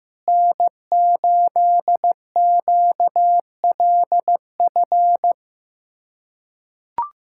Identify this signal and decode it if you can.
Morse code, heard as a keyed tone, about 15 wpm, 700 Hz. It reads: N8QLF
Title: USA Callsign DE Callsign - 5 Characters - 15wpm